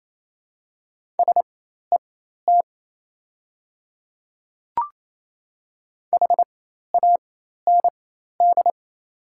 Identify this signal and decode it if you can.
Morse code, heard as a keyed tone, about 28 wpm, spaced as for 13 wpm, 700 Hz. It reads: SET HAND
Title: 4x Character Spacing - Top 300 Words - 28wpm